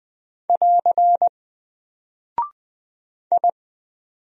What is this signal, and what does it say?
Morse code, heard as a keyed tone, about 20 wpm, 700 Hz. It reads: <AR> I
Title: Single Character (Letter:Number:Punctuation:Prosign) Alt 20wpm